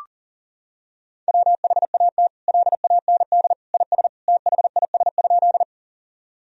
Morse code, heard as a keyed tone, about 40 wpm, 700 Hz. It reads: WHAT LAND IS THIS?
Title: Sentences from Top 1000 Words - Review Part 1 - 40wpm